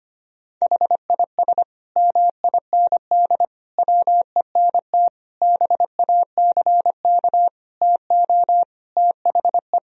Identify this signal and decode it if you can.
Morse code, heard as a keyed tone, about 25 wpm, 700 Hz. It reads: HIS MIND WENT BACK TO THE
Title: Sentences from Top 1000 Words - Review Part 3 - 25wpm